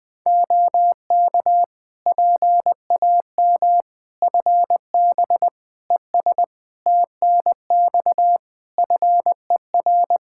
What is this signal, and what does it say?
Morse code, heard as a keyed tone, about 20 wpm, 700 Hz. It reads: OK PAM FB ES TNX FER
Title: K1USN SST QSO Practice 20wpm